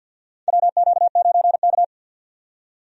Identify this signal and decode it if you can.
Morse code, heard as a keyed tone, about 48 wpm, 700 Hz. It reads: WQ9K
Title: USA Call Signs - 4 Characters - 50wpm